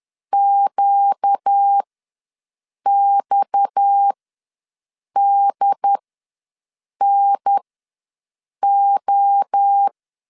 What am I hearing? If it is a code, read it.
Morse code, heard as a keyed tone, about 10 wpm, spaced as for 6 wpm, 800 Hz. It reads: QXDNO
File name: LEZIONI / 6 PKX0 / 6 PKX ws 10wpm 3706